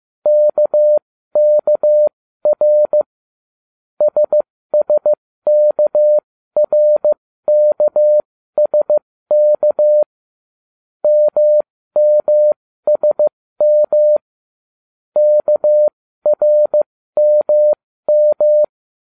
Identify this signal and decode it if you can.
Morse code, heard as a keyed tone, about 15 wpm, spaced as for 13 wpm, 600 Hz. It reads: KKR SSKRKSK MMSM KRMM